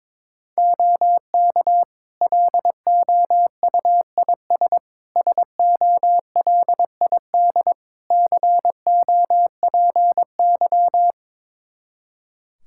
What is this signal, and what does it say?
Morse code, heard as a keyed tone, about 22 wpm, 700 Hz. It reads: OK LOUIS SOLID COPY
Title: Practice QSO - Exchange 3 - More Info - 22wpm